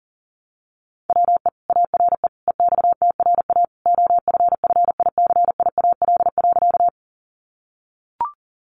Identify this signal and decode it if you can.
Morse code, heard as a keyed tone, about 37 wpm, 700 Hz. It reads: WE ARE EXTRA OFFICIAL.
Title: Sentences from Top 1000 Words 40wpm